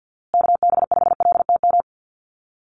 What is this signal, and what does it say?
Morse code, heard as a keyed tone, about 50 wpm, 700 Hz. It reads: KB5LTM
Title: USA Call Signs Repeated 2x 50wpm